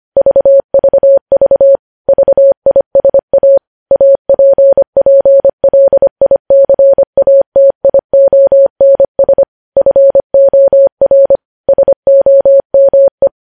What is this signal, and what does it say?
Morse code, heard as a keyed tone, about 25 wpm, 550 Hz. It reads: VVV VISA APPLICATIONS FOR SOME
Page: news headlines in morse code at 25 wpm